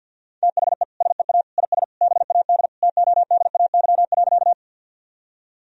Morse code, heard as a keyed tone, about 46 wpm, 700 Hz. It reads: THE SEA IS BAD TODAY.
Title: Sentences from Top 1000 Words - Review Part 1 - 50wpm